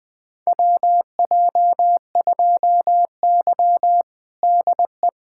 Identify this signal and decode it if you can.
Morse code, heard as a keyed tone, about 21 wpm, 700 Hz. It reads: WJ2Y DE
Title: USA Callsign DE Callsign - 4 Characters - 20wpm